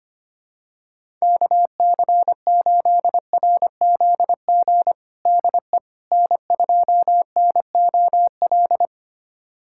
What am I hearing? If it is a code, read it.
Morse code, heard as a keyed tone, about 25 wpm, 700 Hz. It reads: KC8RZG DE N2NOL
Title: USA Callsign DE Callsign 25wpm